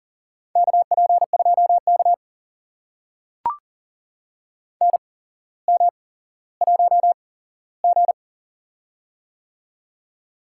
Morse code, heard as a keyed tone, about 40 wpm, spaced as for 17 wpm, 700 Hz. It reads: KP2K NM1G
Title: USA Call Signs - 4 Characters - 8x Character Spacing - 40wpm